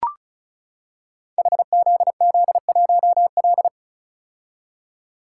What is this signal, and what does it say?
Morse code, heard as a keyed tone, about 34 wpm, 700 Hz. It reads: HZZ1L
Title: Code Groups - 5 Letter:Numbers - 35wpm